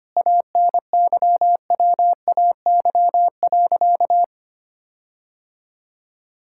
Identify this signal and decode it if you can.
Morse code, heard as a keyed tone, about 25 wpm, 700 Hz. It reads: ANYWAY.
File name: practice / 25wpm / Sentences with 4-Word Phrases - 25wpm